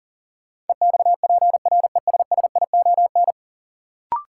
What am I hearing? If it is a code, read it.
Morse code, heard as a keyed tone, about 40 wpm, 700 Hz. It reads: EXPRESSION
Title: Sentences from Top 1200 Words - New Word Drill - 40wpm